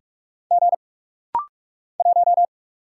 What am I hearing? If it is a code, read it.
Morse code, heard as a keyed tone, about 44 wpm, 700 Hz. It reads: G 1
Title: Single Letter-Number - Rapid-Fire 45wpm